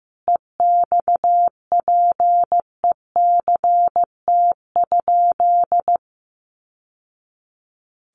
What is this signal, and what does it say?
Morse code, heard as a keyed tone, about 14 wpm, 700 Hz. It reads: EXPECT?
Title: Sentences from Top 1100 Words 15wpm